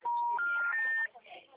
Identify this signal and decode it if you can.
Special information tone, with speech.
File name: out- -2017- - - .39